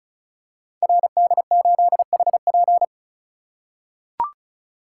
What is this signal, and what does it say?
Morse code, heard as a keyed tone, about 33 wpm, 700 Hz. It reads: RD8HP
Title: Code Groups - 5 Letter:Numbers - 35wpm